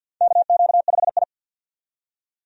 Morse code, heard as a keyed tone, about 53 wpm, 700 Hz. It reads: KQ5I
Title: USA Call Signs - 4 Characters - 50wpm